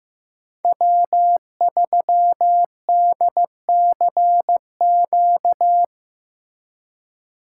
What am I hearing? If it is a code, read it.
Morse code, heard as a keyed tone, about 18 wpm, 700 Hz. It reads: W3DCQ
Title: Speed Racing - USA 1x3 Call Signs 15wpm